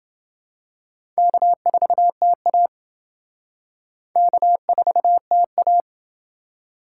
Morse code, heard as a keyed tone, about 28 wpm, 700 Hz. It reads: K4TA K4TA